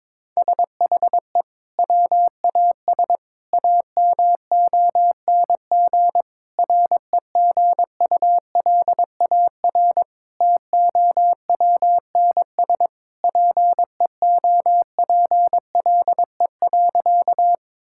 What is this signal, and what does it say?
Morse code, heard as a keyed tone, about 22 wpm, 700 Hz. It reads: SHE WAS AMONG REGULAR TOWNS PEOPLE.
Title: Sentences from Top 1000 Words - Review Part 6 - 22wpm